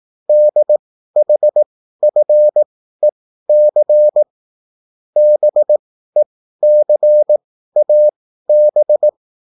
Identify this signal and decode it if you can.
Morse code, heard as a keyed tone, about 18 wpm, spaced as for 13 wpm, 600 Hz. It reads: DHFEC BECAB